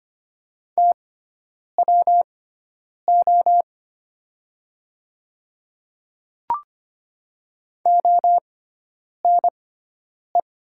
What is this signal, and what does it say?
Morse code, heard as a keyed tone, about 25 wpm, spaced as for 9 wpm, 700 Hz. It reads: TWO ONE
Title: 6x Character Spacing - Top 100 Words - 25wpm